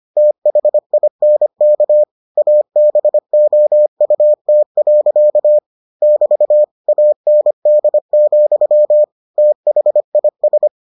Morse code, heard as a keyed tone, about 25 wpm, 600 Hz. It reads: THINK ABOUT. <BT> AND, THIS